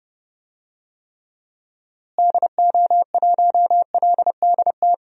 Morse code, heard as a keyed tone, about 30 wpm, 700 Hz. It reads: DO1LDT
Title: DXCC-100 Call Signs - 6 Characters - Repeated 2x - 30wpm